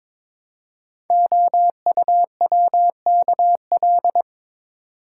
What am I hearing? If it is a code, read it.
Morse code, heard as a keyed tone, about 22 wpm, 700 Hz. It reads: OUWKL
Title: Code Groups - 5 Letters - 22wpm